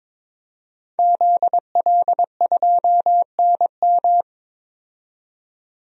Morse code, heard as a keyed tone, about 22 wpm, 700 Hz. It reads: ZL2NM
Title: DXCC-100 Call Signs - 5 Characters - 22wpm